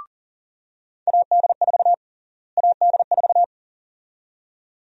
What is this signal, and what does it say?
Morse code, heard as a keyed tone, about 40 wpm, 700 Hz. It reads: AD4 AD4
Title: USA Call Sign Prefixes - Repeated 2x - 40wpm